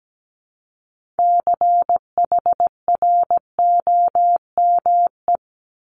Morse code, heard as a keyed tone, about 17 wpm, 690 Hz. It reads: CHROME
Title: Multi-letter Phonemes Repeated 3x 17wpm